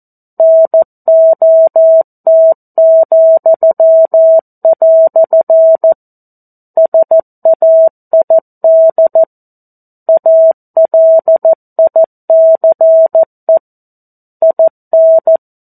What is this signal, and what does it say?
Morse code, heard as a keyed tone, about 14 wpm, 650 Hz. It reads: NOT," SAID ALICE IN
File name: 02-Chapter-02 14 WPM